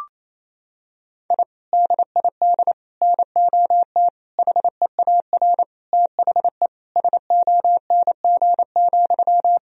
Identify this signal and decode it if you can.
Morse code, heard as a keyed tone, about 28 wpm, 700 Hz. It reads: I DID NOT HEAR THE SONG,
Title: Sentences from Top 1000 Words - Review Part 2 - 28wpm